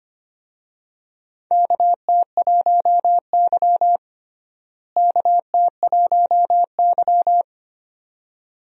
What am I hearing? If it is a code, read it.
Morse code, heard as a keyed tone, about 25 wpm, 700 Hz. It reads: KT1Y KT1Y
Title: USA Call Signs - 4 Characters - Repeated 2x - 25wpm